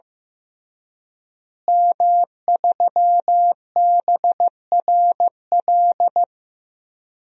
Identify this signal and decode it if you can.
Morse code, heard as a keyed tone, about 15 wpm, 700 Hz. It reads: M3BRL
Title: DXCC-100 Call Signs - 5 Characters - Repeated 3x - 15wpm